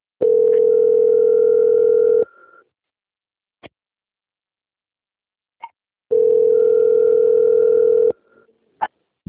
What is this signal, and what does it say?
Ringback tone (North America), 440 + 480 Hz.
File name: out- - - - - .3119